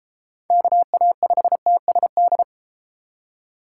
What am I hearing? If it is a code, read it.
Morse code, heard as a keyed tone, about 32 wpm, 700 Hz. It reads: KA5TSD